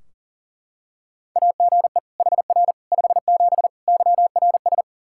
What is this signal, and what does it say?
Morse code, heard as a keyed tone, about 40 wpm, 700 Hz. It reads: AGE HR 57 YRS
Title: K1USN SST QSO Practice 40wpm